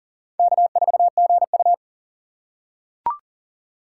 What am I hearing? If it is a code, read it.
Morse code, heard as a keyed tone, about 40 wpm, 700 Hz. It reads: K4GU
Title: USA Call Signs - 6x Character Spacing - 40wpm